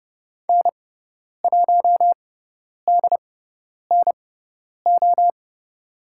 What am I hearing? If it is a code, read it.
Morse code, heard as a keyed tone, about 30 wpm, spaced as for 10 wpm, 700 Hz. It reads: N1DNO
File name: Lesson 16g - Letter c - Call Signs 30wpm